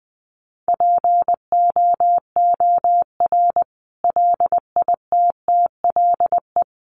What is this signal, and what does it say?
Morse code, heard as a keyed tone, about 19 wpm, 700 Hz. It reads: POOR LITTLE
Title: Sentences from Top 1000 Words 20wpm